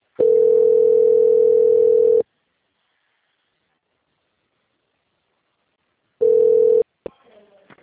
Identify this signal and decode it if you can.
Ringback tone (North America), 440 + 480 Hz.
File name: out- - - - - .34087